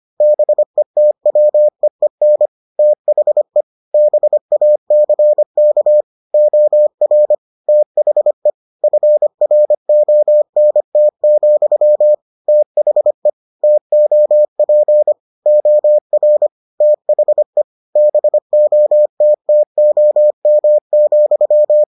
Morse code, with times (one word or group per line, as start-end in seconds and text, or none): 0.2-2.5 BETWEEN
2.8-3.6 THE
3.9-6.0 BACK
6.3-7.3 OR
7.7-8.5 THE
8.8-12.1 FRONT,
12.5-13.3 THE
13.6-15.1 TOP
15.5-16.5 OR
16.8-17.6 THE
18.0-21.8 BOTTOM,